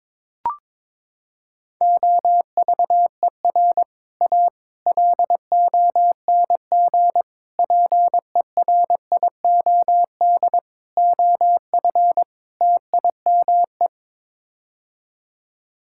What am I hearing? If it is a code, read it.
Morse code, heard as a keyed tone, about 22 wpm, 700 Hz. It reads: OVER A LONG PERIOD OF TIME
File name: practice / 22wpm / Sets of 6 Words - 22wpm